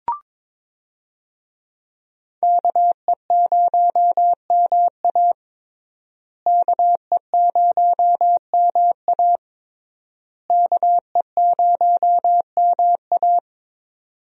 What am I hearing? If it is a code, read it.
Morse code, heard as a keyed tone, about 22 wpm, 700 Hz. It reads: KE0MA KE0MA KE0MA
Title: USA Call Signs - 5 Characters - Repeated 3x - 22wpm